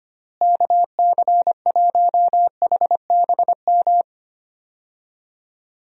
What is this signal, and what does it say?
Morse code, heard as a keyed tone, about 25 wpm, 700 Hz. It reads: KC1HBM
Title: USA Call Signs - 6 Characters - Repeated 3x - 25wpm